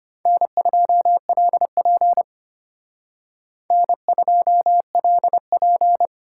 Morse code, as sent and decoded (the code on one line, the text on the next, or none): -. ..--- .-.. .--. / -. ..--- .-.. .--.
N2LP N2LP